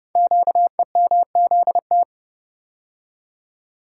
Morse code, heard as a keyed tone, about 30 wpm, 700 Hz. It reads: QEMZT
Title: Code Groups - 5 Letters - 30wpm